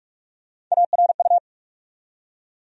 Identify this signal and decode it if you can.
Morse code, heard as a keyed tone, about 43 wpm, 700 Hz. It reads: ARU
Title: USA Call Sign Postfixes - 3 Letters 45wpm